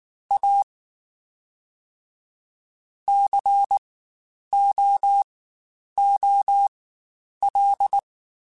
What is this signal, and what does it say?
Morse code, heard as a keyed tone, about 19 wpm, spaced as for 9 wpm, 800 Hz. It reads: A COOL
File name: Spaced Characters / Spaced Characters-mp3s / Weather 5 1 sp @18